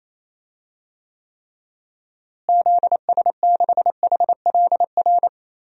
Morse code, heard as a keyed tone, about 28 wpm, 700 Hz. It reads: ZS6HLR
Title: DXCC-100 Call Signs - 6 Characters - Repeated 3x - 28wpm